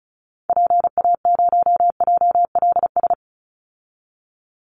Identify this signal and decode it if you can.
Morse code, heard as a keyed tone, about 34 wpm, 700 Hz. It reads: PA0JLS
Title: Call Signs Repeated 2x - 35wpm